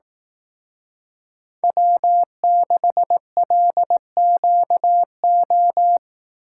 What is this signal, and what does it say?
Morse code, heard as a keyed tone, about 18 wpm, 700 Hz. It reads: W6LQO
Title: Speed Racing - USA Call Signs 15wpm